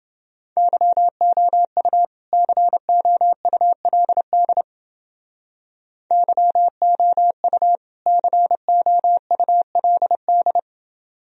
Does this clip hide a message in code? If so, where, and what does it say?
Yes: Morse code from 0.6 to 10.6 s, heard as a keyed tone, about 29 wpm, 700 Hz. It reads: YOU COULD YOU COULD